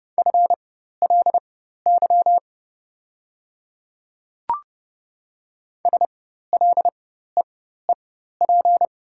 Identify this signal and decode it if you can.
Morse code, heard as a keyed tone, about 29 wpm, spaced as for 14 wpm, 690 Hz. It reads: FLY SLEEP
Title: 4x Character Spacing - Top 500 Words - 30wpm